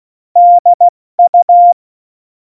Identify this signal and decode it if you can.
Morse code, heard as a keyed tone, about 16 wpm, 700 Hz. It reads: DU
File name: Morsekurs / Witze / Witz 18 August 2020, 70 F80